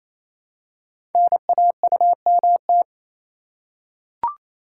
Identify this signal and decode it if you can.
Morse code, heard as a keyed tone, about 27 wpm, 700 Hz. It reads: NAUMT
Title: Code Groups - 5 Letters - 28wpm